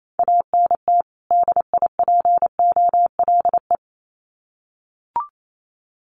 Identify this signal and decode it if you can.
Morse code, heard as a keyed tone, about 28 wpm, 700 Hz. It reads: ANT DIPOLE
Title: Common QSO Phrases 28wpm